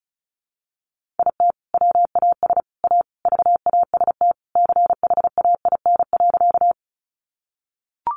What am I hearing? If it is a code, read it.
Morse code, heard as a keyed tone, about 34 wpm, 700 Hz. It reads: IT WAS A VAST CHAIN.
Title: Sentences from Top 1500 Words 35wpm